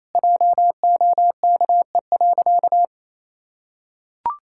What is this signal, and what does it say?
Morse code, heard as a keyed tone, about 28 wpm, 700 Hz. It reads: JOKE.
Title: Pangram Challenge v2 - 28wpm